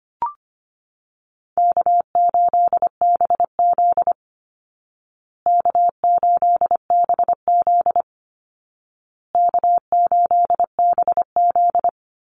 Morse code, heard as a keyed tone, about 25 wpm, 700 Hz. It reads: K8BZ K8BZ K8BZ
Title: Easy Call Signs Repeated 3x - 25wpm